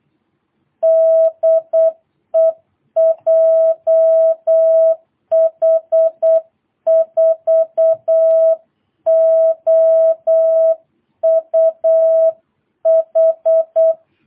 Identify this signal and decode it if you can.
Morse code, heard as a keyed tone, about 8 wpm, 660 Hz. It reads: DE JH4OUH
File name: jh8cbh / 2022cw